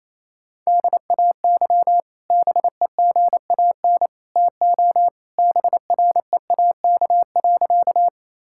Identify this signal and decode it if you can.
Morse code, heard as a keyed tone, about 27 wpm, 700 Hz. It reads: DAY BEGAN TO BREAK.
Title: Sentences from Top 1000 Words - Review Part 6 - 28wpm